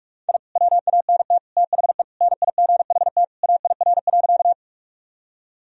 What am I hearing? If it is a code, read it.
Morse code, heard as a keyed tone, about 46 wpm, 700 Hz. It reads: I WANT THE NIGHT AIR.
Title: Sentences from Top 1000 Words - Review Part 1 - 45wpm